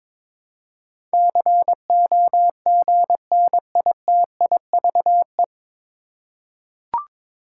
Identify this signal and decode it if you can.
Morse code, heard as a keyed tone, about 23 wpm, 700 Hz. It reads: COGNITIVE
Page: Morse Code Ninja practice 22wpm